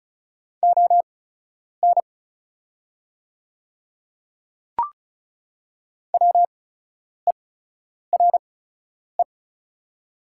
Morse code, heard as a keyed tone, about 34 wpm, spaced as for 10 wpm, 700 Hz. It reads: ON WERE
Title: 8x Character Spacing - Top 100 Words - 35wpm